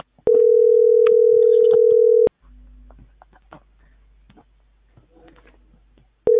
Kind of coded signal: telephone call progress tone